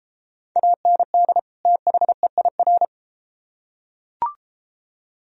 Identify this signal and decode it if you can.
Morse code, heard as a keyed tone, about 31 wpm, 700 Hz. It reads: AND THEIR